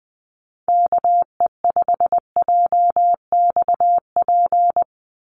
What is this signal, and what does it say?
Morse code, heard as a keyed tone, about 20 wpm, 700 Hz. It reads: KE5JXP
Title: USA Call Signs 20wpm